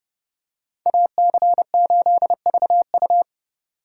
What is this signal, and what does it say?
Morse code, heard as a keyed tone, about 29 wpm, 700 Hz. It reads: AC8VU